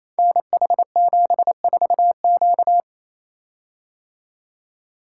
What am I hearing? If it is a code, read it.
Morse code, heard as a keyed tone, about 29 wpm, 700 Hz. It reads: NH74Q
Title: Code Groups - 5 Letter:Numbers - 28wpm